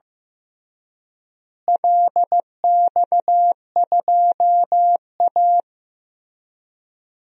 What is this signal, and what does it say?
Morse code, heard as a keyed tone, about 16 wpm, 700 Hz. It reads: LX2A
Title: DXCC-100 Call Signs - 4 Characters - Repeated 2x - 15wpm